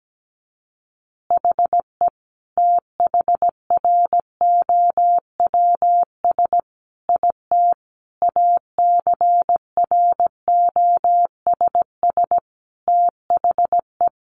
Morse code, heard as a keyed tone, about 17 wpm, 700 Hz. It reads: HE THROWS IT ACROSS THE
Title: Additional Easy Phrases 17wpm